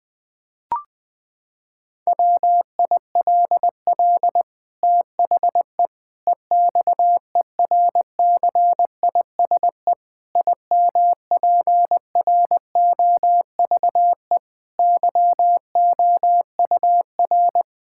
Morse code, heard as a keyed tone, about 21 wpm, 700 Hz. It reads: WILL THE EXERCISE IMPROVE YOUR
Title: Sentences from Top 1000 Words - Review Part 6 - 20wpm